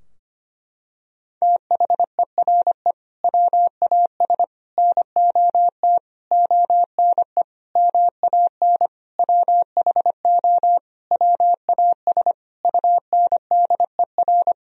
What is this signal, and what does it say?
Morse code, heard as a keyed tone, about 25 wpm, 700 Hz. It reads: THERE WAS NOT ONE MAN WHO WAS UNDER
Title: Sentences from Top 1000 Words - Review Part 2 - 25wpm